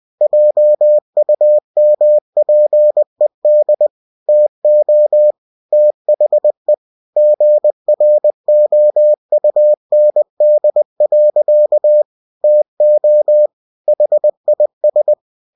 Morse code, heard as a keyed tone, about 20 wpm, 600 Hz. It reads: JUMPED TO THE GROUND. TO HIS